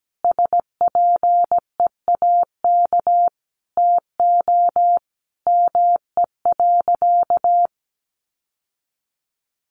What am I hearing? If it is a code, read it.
Morse code, heard as a keyed tone, about 17 wpm, 700 Hz. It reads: SPEAK TO ME.
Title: Sentences from Top 1100 Words 17wpm